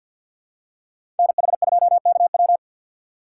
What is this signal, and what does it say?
Morse code, heard as a keyed tone, about 53 wpm, 700 Hz. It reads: NH2KW
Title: USA Call Signs - 5 Characters - 4x Character Spacing - 50wpm